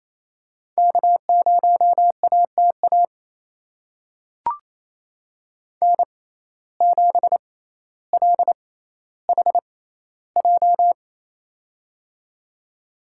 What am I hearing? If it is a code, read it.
Morse code, heard as a keyed tone, about 28 wpm, spaced as for 14 wpm, 700 Hz. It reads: K0ATA N7LHJ
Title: USA Call Signs - 5 Characters - 6x Character Spacing - 28wpm